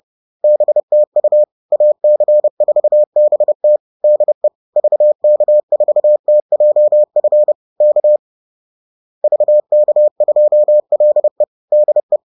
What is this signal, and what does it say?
Morse code, heard as a keyed tone, about 30 wpm, 600 Hz. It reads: BTU AC4BT DE VK4TJF K VK2LE DE